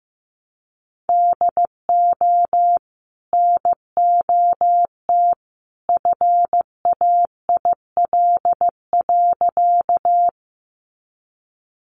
Morse code, heard as a keyed tone, about 15 wpm, 700 Hz. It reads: DO NOT FAIL.
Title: Sentences from Top 1100 Words 15wpm